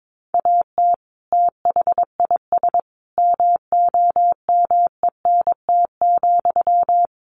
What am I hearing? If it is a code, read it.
Morse code, heard as a keyed tone, about 22 wpm, 700 Hz. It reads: AT THIS MOMENT,
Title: Sentences from Top 1000 Words 22wpm